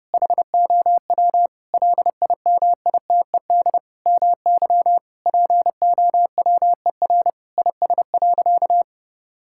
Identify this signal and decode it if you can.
Morse code, heard as a keyed tone, about 32 wpm, 700 Hz. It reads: HOW LIMITED MY POWER IS.
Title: Sentences from Top 1000 Words - Review Part 6 - 30wpm